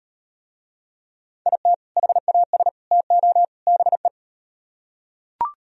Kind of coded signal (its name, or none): Morse code